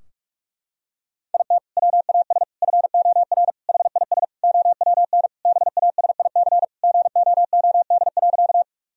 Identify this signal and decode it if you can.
Morse code, heard as a keyed tone, about 48 wpm, 700 Hz. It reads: IT WAS FOR HIS OWN BASIC GOOD.
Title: Sentences from Top 1000 Words - Review Part 3 - 45wpm